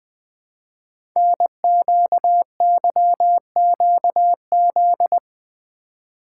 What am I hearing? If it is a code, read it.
Morse code, heard as a keyed tone, about 20 wpm, 700 Hz. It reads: NQYQZ
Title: Code Groups - 5 Letters - 20wpm